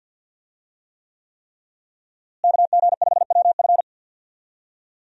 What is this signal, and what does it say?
Morse code, heard as a keyed tone, about 47 wpm, 700 Hz. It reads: KG5WF
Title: USA Call Signs - 5 Characters - Repeated 3x - 50wpm